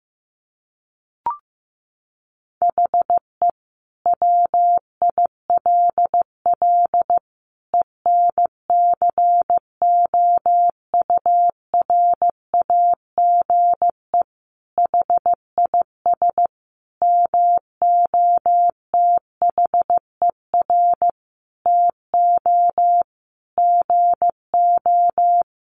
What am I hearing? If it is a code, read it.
Morse code, heard as a keyed tone, about 15 wpm, 700 Hz. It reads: HE WILL ENCOURAGE HIS MOTHER TO GO
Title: Sentences from Top 1400 Words 15wpm